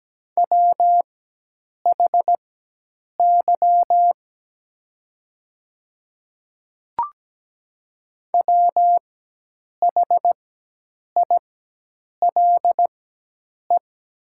Morse code, heard as a keyed tone, about 17 wpm, spaced as for 8 wpm, 700 Hz. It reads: WHY WHILE
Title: 4x Character Spacing - Top 300 Words - 17wpm